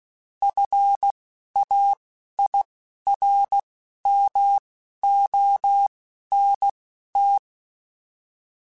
Morse code, heard as a keyed tone, about 16 wpm, spaced as for 11 wpm, 800 Hz. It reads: FAIRMONT